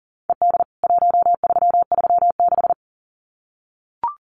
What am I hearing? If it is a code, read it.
Morse code, heard as a keyed tone, about 40 wpm, 700 Hz. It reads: ED 1336
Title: CWT Exchange 40wpm